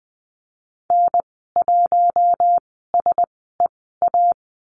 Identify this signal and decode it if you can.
Morse code, heard as a keyed tone, about 20 wpm, spaced as for 14 wpm, 700 Hz. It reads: N1SEA
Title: 042 Lesson 8g - Number 1 - Call Signs 20wpm